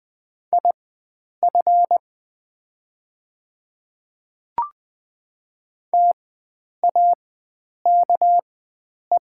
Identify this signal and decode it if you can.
Morse code, heard as a keyed tone, about 20 wpm, spaced as for 9 wpm, 700 Hz. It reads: IF TAKE
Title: 4x Character Spacing - Top 300 Words - 20wpm